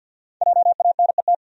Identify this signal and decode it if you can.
Morse code, heard as a keyed tone, about 48 wpm, 700 Hz. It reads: JANET
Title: USA Common Names - Max 5 Letters - 50wpm